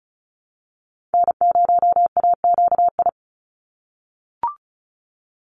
Morse code, heard as a keyed tone, about 34 wpm, 700 Hz. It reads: N0AQI
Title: USA Call Signs 35wpm